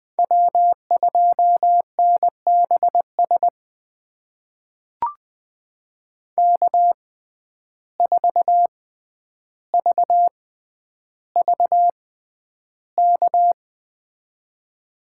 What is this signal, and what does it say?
Morse code, heard as a keyed tone, about 20 wpm, spaced as for 10 wpm, 700 Hz. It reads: W2NBS K4VVK
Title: USA Call Signs - 5 Characters - 6x Character Spacing - 20wpm